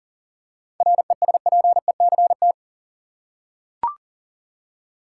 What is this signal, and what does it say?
Morse code, heard as a keyed tone, about 40 wpm, 700 Hz. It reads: RESPECT
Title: Seven Letter Words - 40wpm